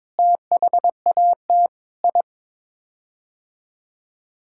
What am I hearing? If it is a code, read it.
Morse code, heard as a keyed tone, about 22 wpm, 700 Hz. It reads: THAT I
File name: practice / 22wpm / Sets of 8 Words - 22wpm